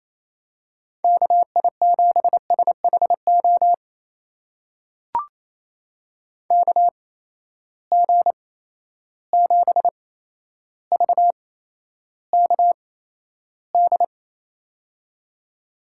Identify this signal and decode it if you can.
Morse code, heard as a keyed tone, about 28 wpm, spaced as for 12 wpm, 700 Hz. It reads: KI7SHO KG7VKD
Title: USA Call Signs - 6 Characters - 8x Character Spacing - 28wpm